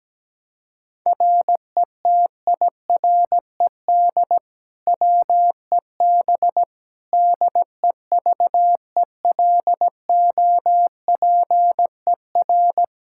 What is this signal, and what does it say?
Morse code, heard as a keyed tone, about 17 wpm, 700 Hz. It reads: RETIRED WEB DEVELOPER
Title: Practice QSO - Exchange 3 - More Info - 17wpm